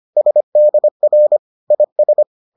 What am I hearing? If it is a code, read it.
Morse code, heard as a keyed tone, about 25 wpm, 600 Hz. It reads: SDR IS